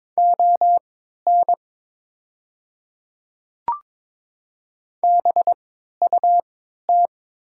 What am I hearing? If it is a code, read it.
Morse code, heard as a keyed tone, about 22 wpm, spaced as for 13 wpm, 690 Hz. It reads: ON BUT
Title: 3x Character Spacing - Top 100 Words - 22wpm